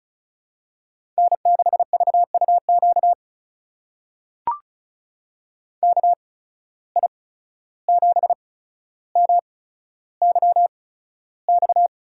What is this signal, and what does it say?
Morse code, heard as a keyed tone, about 36 wpm, spaced as for 14 wpm, 700 Hz. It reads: N6VUQ KI7MYX
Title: USA Call Signs - 8x Character Spacing - 35wpm